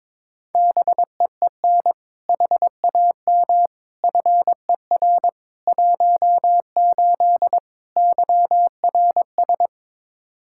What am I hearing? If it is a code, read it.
Morse code, heard as a keyed tone, about 22 wpm, 700 Hz. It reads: BEEN HAM FER 18 YRS
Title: Practice QSO - Exchange 3 - More Info - 22wpm